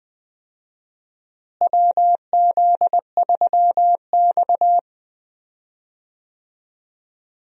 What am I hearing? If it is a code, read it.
Morse code, heard as a keyed tone, about 20 wpm, 700 Hz. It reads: WZ3X
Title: USA Call Signs - 4 Characters - Repeated 2x - 20wpm